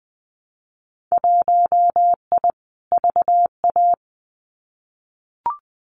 Morse code, heard as a keyed tone, about 20 wpm, 700 Hz. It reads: 1I VA
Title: Winter Field Day Exchanges 20wpm